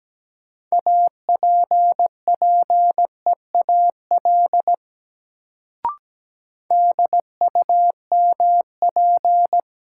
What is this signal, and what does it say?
Morse code, heard as a keyed tone, about 17 wpm, 700 Hz. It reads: APPEAL DUMP
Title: Sentences from Top 1300 Words - New Word Drill Alt - 17wpm